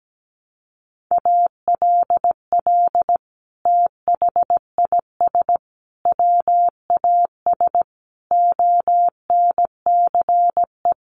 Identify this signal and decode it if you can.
Morse code, heard as a keyed tone, about 17 wpm, 700 Hz. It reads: ALL THIS WAS ONCE
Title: Sentences from Top 1500 Words 17wpm